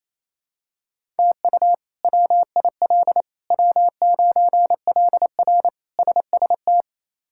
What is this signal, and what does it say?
Morse code, heard as a keyed tone, about 28 wpm, 700 Hz. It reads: TU WIL W9LR SST
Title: K1USN SST QSO Practice - 28wpm